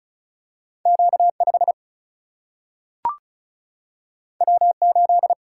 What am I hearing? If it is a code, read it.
Morse code, heard as a keyed tone, about 36 wpm, 700 Hz. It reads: Q5 W8
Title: Single-Letter Single-Number 35wpm